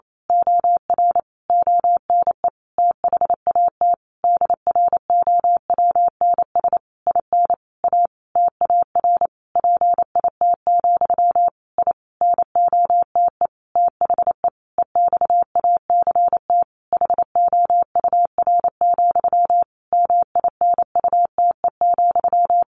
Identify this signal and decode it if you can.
Morse code, heard as a keyed tone, about 28 wpm, 700 Hz. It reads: OR ONE THAT DROWNS IN A TAR PIT, I NOTE THE EXACT HOUR, MINUTE,
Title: A Sound Of Thunder 28wpm